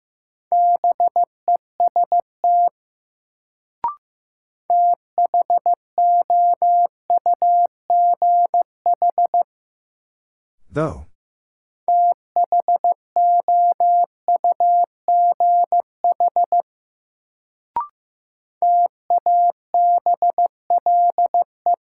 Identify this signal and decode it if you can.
Morse code, heard as a keyed tone, about 15 wpm, 700 Hz. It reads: BEST THOUGH THOUGH TABLE
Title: Sentences from Top 400 Words - New Word Drill - 15wpm